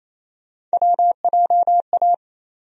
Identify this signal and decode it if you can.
Morse code, heard as a keyed tone, about 28 wpm, 700 Hz. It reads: WJA
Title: USA Call Sign Postfixes - 3 Letters 28wpm